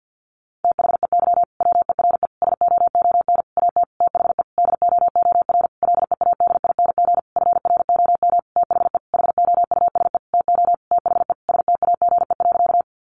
Morse code, heard as a keyed tone, about 50 wpm, 700 Hz. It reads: THEY WERE SOON AT THE DOOR LEADING FROM THE HOUSE TO THE STAGE.
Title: Sentences from Top 1200 Words 50wpm